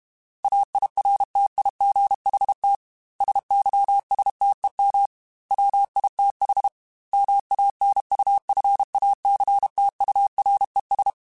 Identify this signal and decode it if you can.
Morse code, heard as a keyed tone, about 32 wpm, 800 Hz. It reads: AIRTIGHT SYSTEM WITH MANUFACTURES